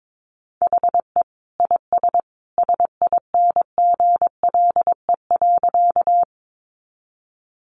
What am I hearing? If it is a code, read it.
Morse code, heard as a keyed tone, about 22 wpm, 700 Hz. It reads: HE IS SINGLE.
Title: Sentences from Top 700 Words 22wpm